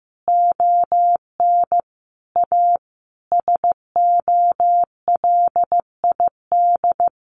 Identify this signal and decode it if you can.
Morse code, heard as a keyed tone, about 14 wpm, 700 Hz. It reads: ON A SOLID
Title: Sentences from Top 1500 Words 15wpm